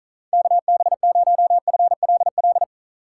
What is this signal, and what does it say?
Morse code, heard as a keyed tone, about 43 wpm, 700 Hz. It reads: KB0FLL
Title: Speed Racing - USA 2x3 Call Signs 35wpm